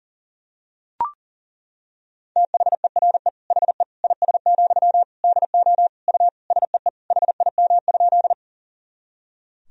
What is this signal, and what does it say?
Morse code, heard as a keyed tone, about 40 wpm, 700 Hz. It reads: THERE HE IS, DO U SEE HIM?
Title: Sentences from Top 100 Words and Abbreviations 40wpm